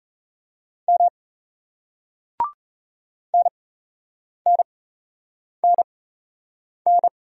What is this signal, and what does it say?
Morse code, heard as a keyed tone, about 38 wpm, spaced as for 8 wpm, 700 Hz. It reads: M NNNN